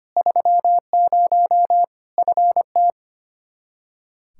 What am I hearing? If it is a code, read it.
Morse code, heard as a keyed tone, about 25 wpm, 700 Hz. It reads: 30 FT
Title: Practice QSO - Exchange 2 - Info - 25wpm